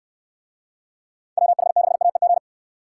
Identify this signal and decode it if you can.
Morse code, heard as a keyed tone, about 66 wpm, 700 Hz. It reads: WH6AEG